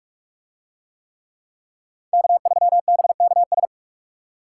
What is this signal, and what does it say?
Morse code, heard as a keyed tone, about 41 wpm, 700 Hz. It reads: K3BKS